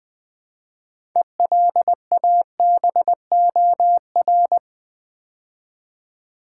Morse code, heard as a keyed tone, about 20 wpm, 700 Hz. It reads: ELABOR
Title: Build A Word - 3 Letters At A Time - 9-Letter Words - 20wpm